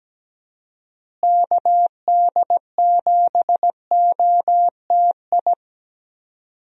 Morse code, heard as a keyed tone, about 17 wpm, 700 Hz. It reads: KD7OTI
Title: USA Call Signs - 6 Characters - Repeated 2x - 17wpm